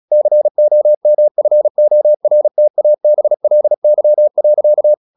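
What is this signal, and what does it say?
Morse code, heard as a keyed tone, about 36 wpm, 600 Hz. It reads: COMFORTABLY.
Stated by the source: Morse practice at 35 wpm from CW med Gnister, sending Ken Follett's novel Never in English.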